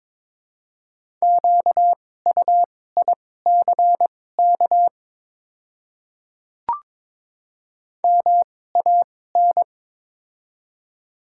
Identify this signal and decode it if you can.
Morse code, heard as a keyed tone, about 22 wpm, spaced as for 16 wpm, 700 Hz. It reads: QUICK MAN
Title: 2x Character Spacing - Top 500 Words - 22wpm